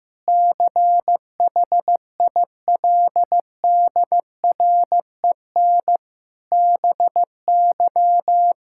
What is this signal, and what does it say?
Morse code, heard as a keyed tone, about 16 wpm, 700 Hz. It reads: CHILDREN BY
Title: Sentences from Top 1000 Words - Review Part 5 - 15wpm